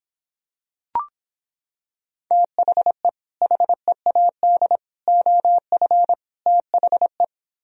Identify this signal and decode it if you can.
Morse code, heard as a keyed tone, about 26 wpm, 700 Hz. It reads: THE HEAD OF THE